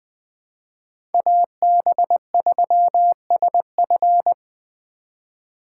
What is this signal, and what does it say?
Morse code, heard as a keyed tone, about 22 wpm, 700 Hz. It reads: AB3SF